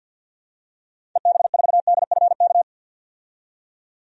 Morse code, heard as a keyed tone, about 51 wpm, 700 Hz. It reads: EB4DFK
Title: DXCC-100 Call Signs - 6 Characters - 50wpm